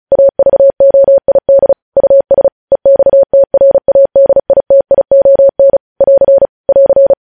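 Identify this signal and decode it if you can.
Morse code, heard as a keyed tone, about 36 wpm, 550 Hz. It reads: AVOID US EXTRADITION <AR> <AR>
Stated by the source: Morse news headlines practice at 35.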